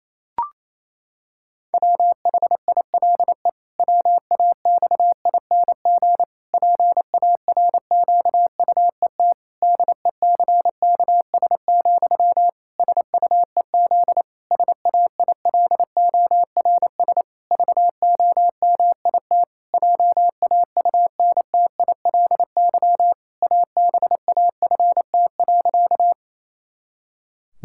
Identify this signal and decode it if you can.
Morse code, heard as a keyed tone, about 28 wpm, 700 Hz. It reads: WHILE WAXING PARQUET DECKS, SUEZ SAILORS VOMIT JAUNTILY ABAFT.
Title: Pangram Challenge v2 - 28wpm